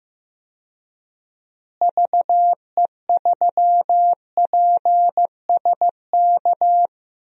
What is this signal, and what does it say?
Morse code, heard as a keyed tone, about 15 wpm, 700 Hz. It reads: VE3PSK
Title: DXCC-100 Call Signs - 6 Characters - Repeated 3x - 15wpm